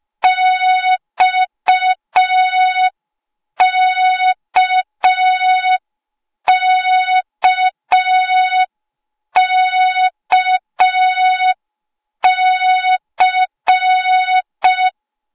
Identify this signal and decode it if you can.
Morse code, heard as a keyed tone, about 5 wpm, 3700 Hz. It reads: XKKKC
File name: Morze-HA2MN / hang / 08KXC HA2MN